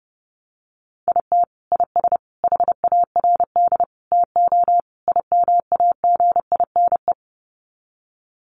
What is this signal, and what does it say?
Morse code, heard as a keyed tone, about 30 wpm, 700 Hz. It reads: IT IS HARD TO IMAGINE
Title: Sets of 5 Words Encore2 30wpm